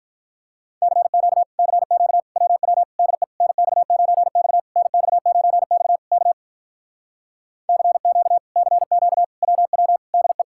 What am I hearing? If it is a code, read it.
Morse code, heard as a keyed tone, about 46 wpm, 700 Hz. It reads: CQ CQ WW DE NX9X NX9X K CQ CQ WW DE